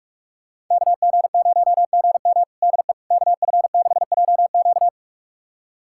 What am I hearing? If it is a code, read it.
Morse code, heard as a keyed tone, about 46 wpm, 700 Hz. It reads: KG0GM DE KF6JQ
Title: USA Callsign DE Callsign - 5 Characters - 45wpm